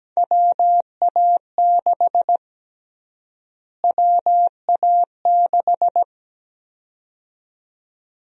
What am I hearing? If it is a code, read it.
Morse code, heard as a keyed tone, about 17 wpm, 700 Hz. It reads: WA6 WA6
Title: USA Call Sign Prefixes - Repeated 2x - 17wpm